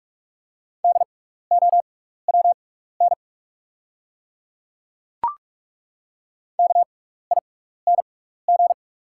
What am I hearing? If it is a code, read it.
Morse code, heard as a keyed tone, about 48 wpm, spaced as for 16 wpm, 690 Hz. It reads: DOWN KING
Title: 6x Character Spacing - Top 500 Words - 45wpm